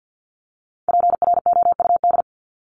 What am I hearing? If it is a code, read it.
Morse code, heard as a keyed tone, about 48 wpm, 700 Hz. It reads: PROUD
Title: Top 1501-1750 Words 50wpm